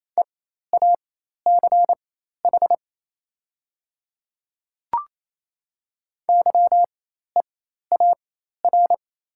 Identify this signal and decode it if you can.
Morse code, heard as a keyed tone, about 27 wpm, spaced as for 13 wpm, 700 Hz. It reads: EACH YEAR
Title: 4x Character Spacing - Top 200 Words - 28wpm